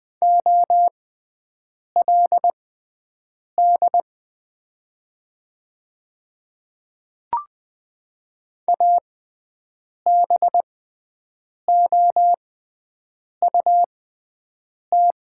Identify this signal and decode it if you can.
Morse code, heard as a keyed tone, about 20 wpm, spaced as for 7 wpm, 700 Hz. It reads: OLD ABOUT
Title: 6x Character Spacing - Top 200 Words - 20wpm